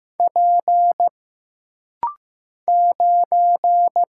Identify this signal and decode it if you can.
Morse code, heard as a keyed tone, about 15 wpm, 700 Hz. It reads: P 9
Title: Single Letter-Number Alt - Rapid-Fire 15wpm